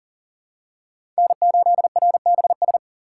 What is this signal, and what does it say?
Morse code, heard as a keyed tone, about 40 wpm, 700 Hz. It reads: N8RBS
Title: USA Call Signs - 5 Characters - 2x Character Spacing - 40wpm